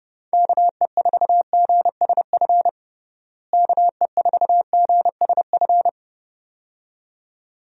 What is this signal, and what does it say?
Morse code, heard as a keyed tone, about 31 wpm, 700 Hz. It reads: KE4GSF KE4GSF
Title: USA Call Signs - 6 Characters - Repeated 2x - 30wpm